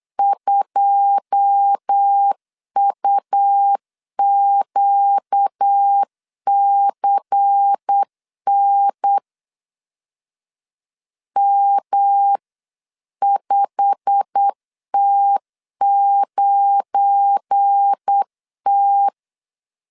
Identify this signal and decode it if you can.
Morse code, heard as a keyed tone, about 8 wpm, 800 Hz. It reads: 2UQCN M5T9T
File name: CW / DET / 1 9 1500C S C 8wpm 5653